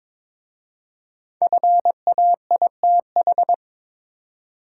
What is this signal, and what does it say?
Morse code, heard as a keyed tone, about 22 wpm, 700 Hz. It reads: FAITH
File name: 5 Letter Words - First 3 Letters Given - 22wpm